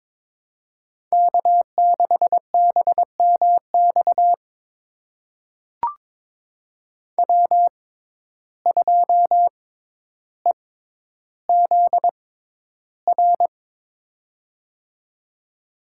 Morse code, heard as a keyed tone, about 22 wpm, spaced as for 11 wpm, 700 Hz. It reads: K6BMX W2EZR
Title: USA Call Signs - 5 Characters - 6x Character Spacing - 22wpm